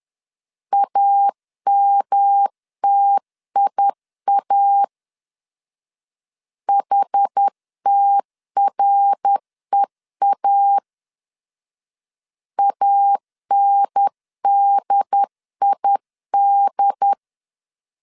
Morse code, heard as a keyed tone, about 10 wpm, 800 Hz. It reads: AMTIA HTREA ANDID
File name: 10wpm